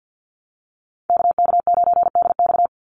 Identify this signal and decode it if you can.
Morse code, heard as a keyed tone, about 48 wpm, 700 Hz. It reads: KK9DX
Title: Call Signs Repeated 3x - 50wpm